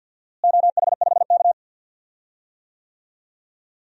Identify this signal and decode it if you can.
Morse code, heard as a keyed tone, about 47 wpm, 700 Hz. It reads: OH5K
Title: DXCC-100 Call Signs - 4 Characters - Repeated 2x - 50wpm